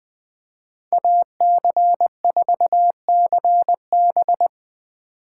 Morse code, heard as a keyed tone, about 20 wpm, 700 Hz. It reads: AC4CB
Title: Speed Racing - USA 2x2 Call Signs 17wpm